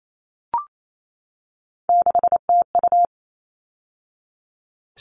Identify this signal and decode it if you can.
Morse code, heard as a keyed tone, about 28 wpm, 700 Hz. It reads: 6TU
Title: All Cut Numbers - 3-Digit 28wpm